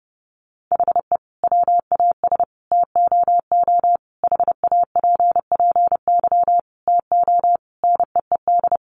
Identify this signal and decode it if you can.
Morse code, heard as a keyed tone, about 30 wpm, 700 Hz. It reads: HE WAS TOO HAPPY TO NEED
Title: Sentences from Top 1000 Words 30wpm